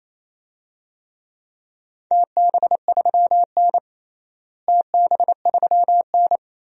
Morse code, heard as a keyed tone, about 28 wpm, 700 Hz. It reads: TB3N TB3N
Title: DXCC-100 Call Signs - 4 Characters - Repeated 2x - 28wpm